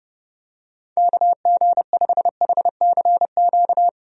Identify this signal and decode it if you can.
Morse code, heard as a keyed tone, about 30 wpm, 700 Hz. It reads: KG5HCQ
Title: USA Call Signs - 6 Characters - Repeated 2x - 30wpm